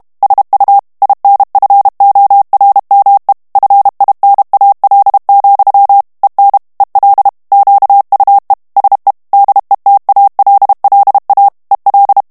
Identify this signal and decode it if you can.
Morse code, heard as a keyed tone, about 32 wpm, 800 Hz. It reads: SU INFORME FINAL, EN EL QUE SE DETALLA EL
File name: morse mp3 / ESPAGNOL TEXTE EN CLAIR 1 VITESSE 30 MOTS MINUTES